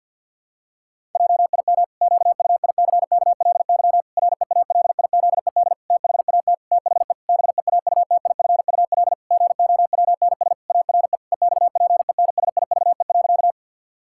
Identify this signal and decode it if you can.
Morse code, heard as a keyed tone, about 49 wpm, 700 Hz. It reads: JIM QUICKLY REALIZED THAT THE BEAUTIFUL GOWNS ARE EXPENSIVE.
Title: Pangram Challenge v2 - 50wpm